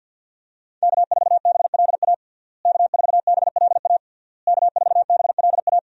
Morse code, heard as a keyed tone, about 52 wpm, 700 Hz. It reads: K4BLA K4BLA K4BLA
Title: USA Call Signs - 5 Characters - Repeated 3x - 50wpm